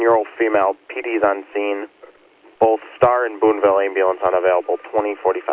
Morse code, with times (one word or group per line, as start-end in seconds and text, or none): none